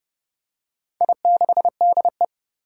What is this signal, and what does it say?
Morse code, heard as a keyed tone, about 29 wpm, 700 Hz. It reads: I6DE
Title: Four Challenging Letter-Number Combinations 30wpm